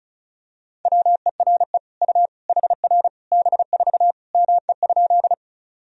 Morse code, heard as a keyed tone, about 34 wpm, 700 Hz. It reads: WERE U HR B4 ME?
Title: Sentences from Top 200 Words and Abbreviations 35wpm